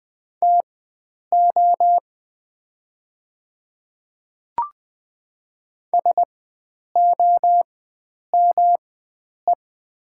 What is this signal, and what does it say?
Morse code, heard as a keyed tone, about 20 wpm, spaced as for 9 wpm, 700 Hz. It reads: TO SOME
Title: 4x Character Spacing - Top 100 Words - 20wpm